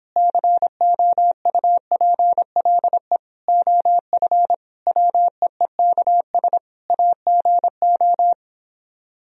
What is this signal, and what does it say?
Morse code, heard as a keyed tone, about 29 wpm, 700 Hz. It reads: COUPLE OF WEEKS AGO